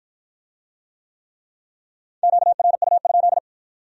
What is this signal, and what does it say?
Morse code, heard as a keyed tone, about 53 wpm, 700 Hz. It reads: QRU?